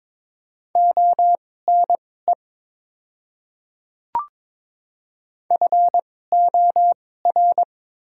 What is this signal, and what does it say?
Morse code, heard as a keyed tone, about 22 wpm, spaced as for 16 wpm, 690 Hz. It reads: ONE FOR
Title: 2x Character Spacing - Top 100 Words - 22wpm